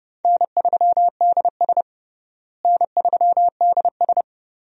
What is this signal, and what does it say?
Morse code, heard as a keyed tone, about 31 wpm, 700 Hz. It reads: N3DS N3DS
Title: USA Call Signs - 4 Characters - Repeated 2x - 30wpm